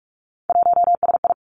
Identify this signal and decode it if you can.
Morse code, heard as a keyed tone, about 46 wpm, 700 Hz. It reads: 1SI
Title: Three Challenging Letter-Number Combinations 45wpm